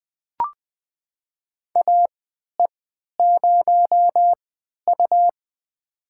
Morse code, heard as a keyed tone, about 20 wpm, spaced as for 11 wpm, 700 Hz. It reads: AE0U
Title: USA Call Signs - 4 Characters - 3x Character Spacing - 20wpm